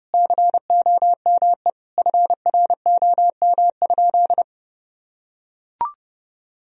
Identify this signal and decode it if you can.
Morse code, heard as a keyed tone, about 30 wpm, 700 Hz. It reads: COME FROM?
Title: Sentences from Top 1000 Words - Review Part 1 - 30wpm